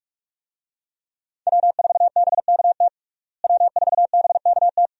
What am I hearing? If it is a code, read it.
Morse code, heard as a keyed tone, about 47 wpm, 700 Hz. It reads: W4BKT W4BKT
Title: USA Call Signs - 5 Characters - Repeated 2x - 45wpm